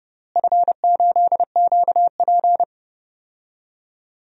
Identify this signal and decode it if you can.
Morse code, heard as a keyed tone, about 30 wpm, 700 Hz. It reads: F8QP
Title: DXCC-100 Call Signs - 4 Characters - Repeated 3x - 30wpm